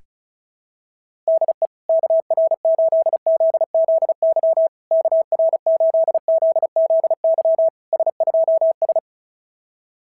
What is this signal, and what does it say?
Morse code, heard as a keyed tone, about 33 wpm, 660 Hz. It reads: DE KR8ZZY KR8ZZY S2S
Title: SOTA QSO Practice 35wpm